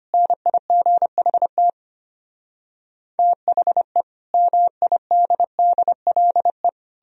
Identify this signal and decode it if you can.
Morse code, heard as a keyed tone, about 33 wpm, 700 Hz. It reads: NIGHT THE MIDDLE